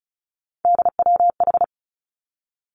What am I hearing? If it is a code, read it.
Morse code, heard as a keyed tone, about 36 wpm, 700 Hz. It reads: DWH
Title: Three Challenging Letter-Number Combinations 35wpm